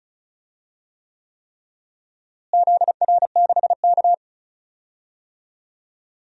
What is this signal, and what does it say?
Morse code, heard as a keyed tone, about 36 wpm, 700 Hz. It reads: ZR6K
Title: DXCC-100 Call Signs - 4 Characters - Repeated 3x - 35wpm